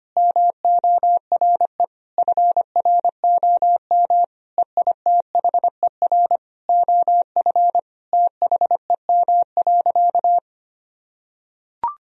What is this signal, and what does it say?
Morse code, heard as a keyed tone, about 25 wpm, 700 Hz. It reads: MORE FROM EITHER OF THEM.
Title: Sentences from Top 1000 Words - Review Part 4 - 25wpm